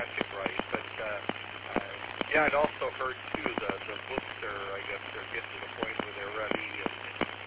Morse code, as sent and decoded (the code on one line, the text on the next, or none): none